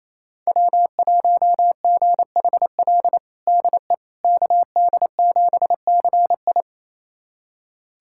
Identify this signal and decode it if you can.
Morse code, heard as a keyed tone, about 28 wpm, 700 Hz. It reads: W1GHL DE KD7CI
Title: USA Callsign DE Callsign - 5 Characters - 28wpm